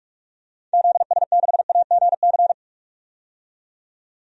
Morse code, heard as a keyed tone, about 42 wpm, 700 Hz. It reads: ZS6AGC
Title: DXCC-100 Call Signs - 6 Characters - 45wpm